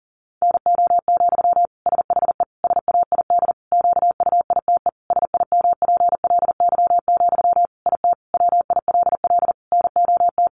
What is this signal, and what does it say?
Morse code, heard as a keyed tone, about 40 wpm, 700 Hz. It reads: NO, SHE SAID QUITE SIMPLY, IT WILL NOT
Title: Sentences from Top 800 Words 40wpm